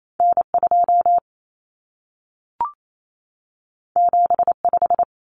Morse code, heard as a keyed tone, about 28 wpm, 700 Hz. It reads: N2 75
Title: Common Cut Numbers - 2-Digit 28wpm